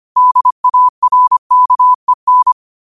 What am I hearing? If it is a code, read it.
Morse code, heard as a keyed tone, about 25 wpm, 1000 Hz. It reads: DARKEN
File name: qotdfiles / 25wpm